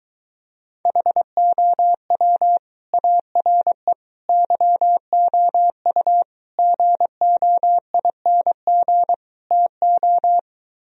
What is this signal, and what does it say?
Morse code, heard as a keyed tone, about 23 wpm, 700 Hz. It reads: HOW ARE YOU GOING TO